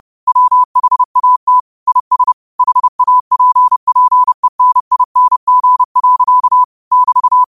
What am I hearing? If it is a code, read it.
Morse code, heard as a keyed tone, about 30 wpm, 1000 Hz. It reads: WHAT IS HAPPENING. <BT>
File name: qotdfiles / 30wpm